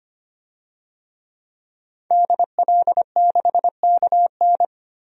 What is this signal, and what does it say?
Morse code, heard as a keyed tone, about 25 wpm, 700 Hz. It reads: DL6KN
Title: DXCC-100 Call Signs - 5 Characters - 25wpm